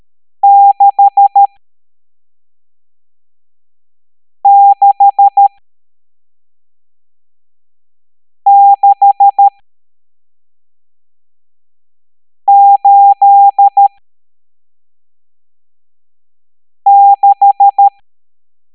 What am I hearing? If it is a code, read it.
Morse code, heard as a keyed tone, about 13 wpm, spaced as for 3 wpm, 800 Hz. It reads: 66686
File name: hamdata / L11-EX01-67890